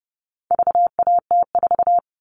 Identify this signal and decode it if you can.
Morse code, heard as a keyed tone, about 30 wpm, 700 Hz. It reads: VAT4
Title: All Cut Numbers - 4-Digit 30wpm